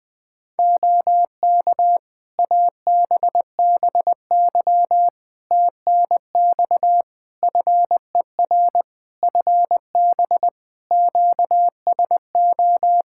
Morse code, heard as a keyed tone, about 20 wpm, 700 Hz. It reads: OK ABBY TNX FER FB QSO
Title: Practice QSO - Exchange 4 - Ending - 20wpm